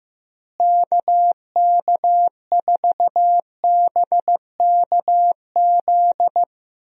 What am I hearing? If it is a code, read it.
Morse code, heard as a keyed tone, about 15 wpm, 700 Hz. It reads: KK4BKZ
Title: USA Call Signs - 6 Characters - Repeated 3x - 15wpm